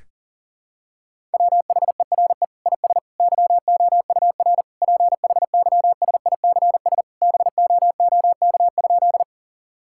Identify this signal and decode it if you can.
Morse code, heard as a keyed tone, about 40 wpm, 700 Hz. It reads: WHERE IS YOUR PHYSICS BOOK?
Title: Sentences from Top 1000 Words - Review Part 5 - 40wpm